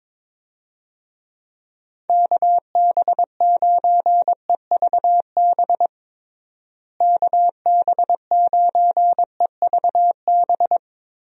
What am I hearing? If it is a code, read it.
Morse code, heard as a keyed tone, about 22 wpm, 700 Hz. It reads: KB9EVB KB9EVB
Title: USA Call Signs - 6 Characters - Repeated 3x - 22wpm